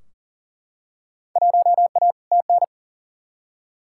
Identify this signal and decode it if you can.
Morse code, heard as a keyed tone, about 40 wpm, 700 Hz. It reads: 1A TN
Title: Field Day Exchanges - 40wpm